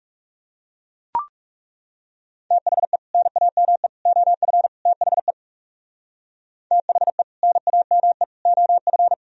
Morse code, heard as a keyed tone, about 37 wpm, 700 Hz. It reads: THE NAME OF THE THE NAME OF
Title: Speed-Racing - Sets of 4 Words 30wpm